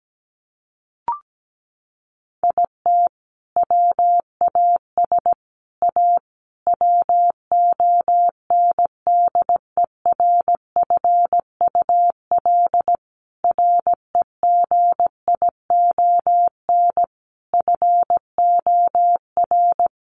Morse code, heard as a keyed tone, about 17 wpm, 700 Hz. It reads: IT WAS A WONDERFUL REGION FOR
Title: Sentences from Top 1500 Words 17wpm